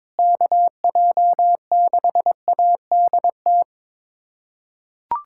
Morse code, heard as a keyed tone, about 21 wpm, 700 Hz. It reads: KJ6ADT
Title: USA Callsign DE Callsign 22wpm